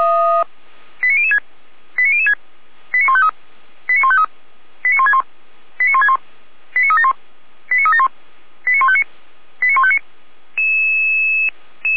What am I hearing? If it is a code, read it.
Five-tone selective call (ZVEI), heard as quick runs of five tones, tones of about 70 ms. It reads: 89006 89006 89153 89153 89161 89161 89371 89371 89159 89159